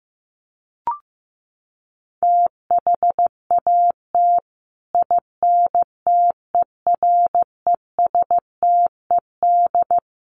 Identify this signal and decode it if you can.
Morse code, heard as a keyed tone, about 15 wpm, 700 Hz. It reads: THAT INTERESTED
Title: Sentences from Top 900 Words 15wpm